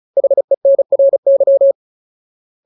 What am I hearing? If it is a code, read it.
Morse code, heard as a keyed tone, about 37 wpm, 550 Hz. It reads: HENRY
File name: Ultimate Word List / 35WPM Ultimate